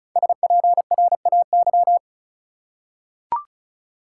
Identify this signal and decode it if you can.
Morse code, heard as a keyed tone, about 34 wpm, 700 Hz. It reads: SPRAY
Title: Five Letter Words - 35wpm